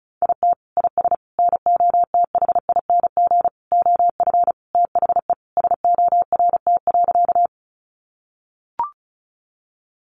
Morse code, heard as a keyed tone, about 34 wpm, 700 Hz. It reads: IT IS NOTHING OF THE SORT.
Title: Sentences from Top 1300 Words 35wpm